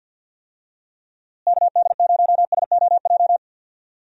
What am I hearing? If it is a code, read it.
Morse code, heard as a keyed tone, about 51 wpm, 700 Hz. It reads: KD0SOJ
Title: USA Call Signs - 6 Characters - Repeated 2x - 50wpm